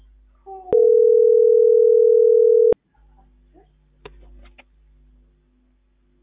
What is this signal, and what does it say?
Ringback tone (North America), 440 + 480 Hz.